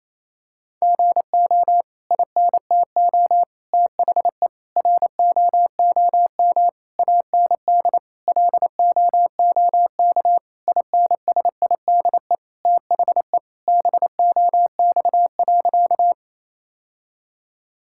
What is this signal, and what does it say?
Morse code, heard as a keyed tone, about 28 wpm, 700 Hz. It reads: GO INTO THE ROOM AND LOOK INSIDE THE BOX.
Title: Sentences from Top 1000 Words - Review Part 6 - 28wpm